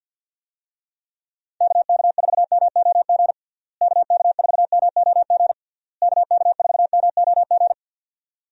Morse code, heard as a keyed tone, about 51 wpm, 700 Hz. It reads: KK4MOG KK4MOG KK4MOG
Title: USA Call Signs - 6 Characters - Repeated 3x - 50wpm